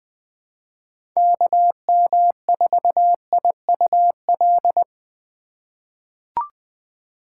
Morse code, heard as a keyed tone, about 20 wpm, 700 Hz. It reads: KM4IUL
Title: USA Call Signs - 6 Characters - 4x Character Spacing - 20wpm